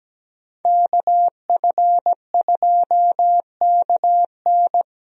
Morse code, heard as a keyed tone, about 17 wpm, 700 Hz. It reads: KF2KN
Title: USA Call Signs - 5 Characters - Repeated 2x - 17wpm